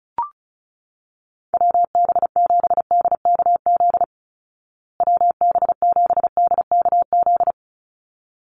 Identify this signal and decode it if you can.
Morse code, heard as a keyed tone, about 34 wpm, 700 Hz. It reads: WB7DKZ WB7DKZ
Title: Call Signs Repeated 3x - 35wpm